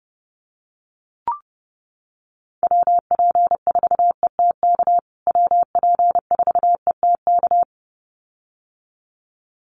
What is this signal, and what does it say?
Morse code, heard as a keyed tone, about 30 wpm, 700 Hz. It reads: WP4ETK WP4ETK
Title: USA Call Signs Repeated 2x 30wpm